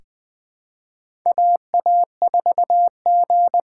Morse code, heard as a keyed tone, about 20 wpm, 700 Hz. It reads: AA4G
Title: USA Call Signs - 6x Character Spacing - 20wpm